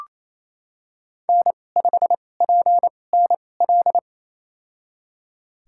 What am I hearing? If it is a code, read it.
Morse code, heard as a keyed tone, about 28 wpm, spaced as for 20 wpm, 700 Hz. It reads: N5PNL
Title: USA Call Signs - 5 Characters - 2x Character Spacing - 28wpm